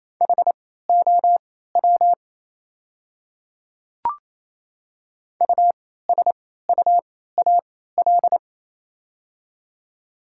Morse code, heard as a keyed tone, about 28 wpm, spaced as for 16 wpm, 700 Hz. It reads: HOW USUAL
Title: 3x Character Spacing - Top 500 Words - 28wpm